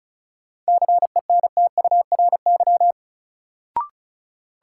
Morse code, heard as a keyed tone, about 34 wpm, 700 Hz. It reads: CENTURY
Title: Sets of 5 Words - New Word Drill - 35wpm